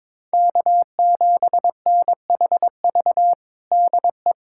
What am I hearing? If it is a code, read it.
Morse code, heard as a keyed tone, about 22 wpm, 700 Hz. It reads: K7NHV DE
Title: USA Callsign DE Callsign 22wpm